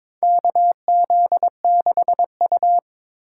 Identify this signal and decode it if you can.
Morse code, heard as a keyed tone, about 22 wpm, 700 Hz. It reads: KZ6U
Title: USA Call Signs - 4 Characters - 2x Character Spacing - 22wpm